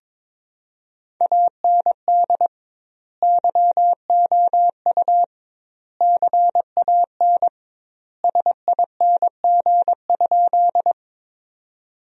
Morse code, heard as a keyed tone, about 22 wpm, 700 Hz. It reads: AND YOU CAN SING?
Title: 2x Word-Spacing - Sentences from Top 400 Words - 22wpm